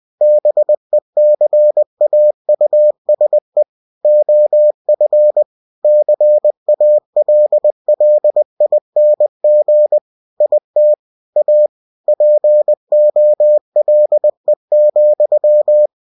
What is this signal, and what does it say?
Morse code, heard as a keyed tone, about 20 wpm, 600 Hz. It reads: BECAUSE OF CALLING IT A POLE,
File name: cw-books / winnie-the-pooh / winnie-20wpm-page-0085